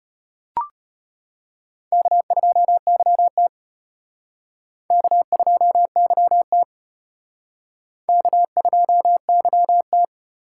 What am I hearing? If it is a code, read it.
Morse code, heard as a keyed tone, about 34 wpm, 700 Hz. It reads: K2YT K2YT K2YT